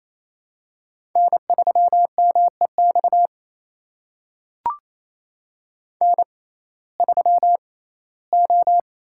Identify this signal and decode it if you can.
Morse code, heard as a keyed tone, about 28 wpm, spaced as for 17 wpm, 700 Hz. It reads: N3MEX N3O
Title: USA Call Signs - 5 Characters - 6x Character Spacing - 28wpm